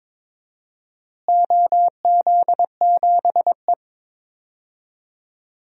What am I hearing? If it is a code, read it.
Morse code, heard as a keyed tone, about 22 wpm, 700 Hz. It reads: OZ7E
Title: DXCC-100 Call Signs - 4 Characters - Repeated 3x - 22wpm